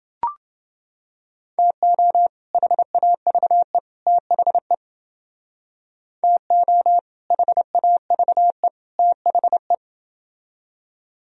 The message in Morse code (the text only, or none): TO HAVE THE TO HAVE THE